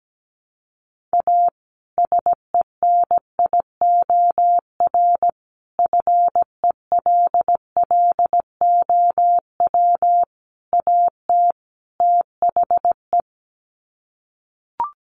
Morse code, heard as a keyed tone, about 17 wpm, 700 Hz. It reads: A SENIOR FELLOW AT THE
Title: Sets of 5 Words Encore 17wpm